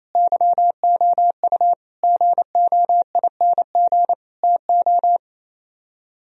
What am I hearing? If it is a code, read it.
Morse code, heard as a keyed tone, about 35 wpm, 700 Hz. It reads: YOU GOING TO
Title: Speed-Racing - Sets of 4 Words 28wpm